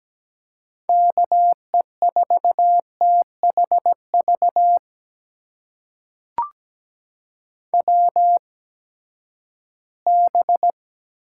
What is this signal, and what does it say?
Morse code, heard as a keyed tone, about 17 wpm, spaced as for 12 wpm, 700 Hz. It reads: KE4THV WB
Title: USA Call Signs - 6 Characters - 8x Character Spacing - 17wpm